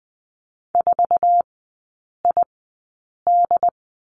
Morse code, heard as a keyed tone, about 20 wpm, spaced as for 8 wpm, 700 Hz. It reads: 4ID
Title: 067 Lesson 12e - Letter d - Three Characters 20wpm